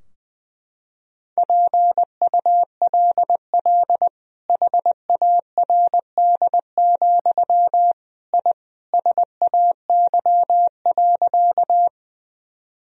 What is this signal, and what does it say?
Morse code, heard as a keyed tone, about 20 wpm, 700 Hz. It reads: PULL HARD, I SAY.
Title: Sentences from Top 1000 Words - Review Part 2 - 20wpm